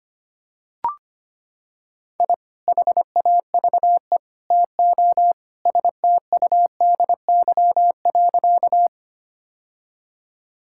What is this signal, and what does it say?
Morse code, heard as a keyed tone, about 25 wpm, 700 Hz. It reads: I HAVE TO STUDY.
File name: practice / 25wpm / Sentences with 3-Word Phrases - 25wpm